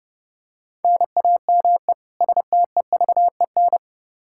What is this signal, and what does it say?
Morse code, heard as a keyed tone, about 30 wpm, 700 Hz. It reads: NAME STEVEN
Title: Practice QSO - Exchange 1 - Beginning - 30wpm